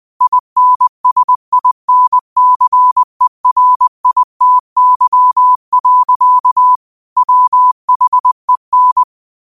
Morse code, heard as a keyed tone, about 20 wpm, 1000 Hz. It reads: INSINCERITY. WHEN
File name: qotdfiles / 20wpm